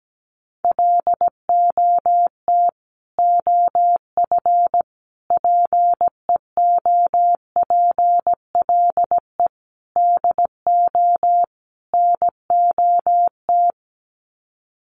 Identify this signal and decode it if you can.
Morse code, heard as a keyed tone, about 17 wpm, 700 Hz. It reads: LOT OF PEOPLE DO NOT
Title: Sets of 5 Words Encore 17wpm